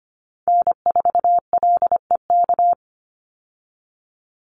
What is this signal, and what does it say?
Morse code, heard as a keyed tone, about 25 wpm, 700 Hz. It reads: N4LEK
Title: USA Call Signs 25wpm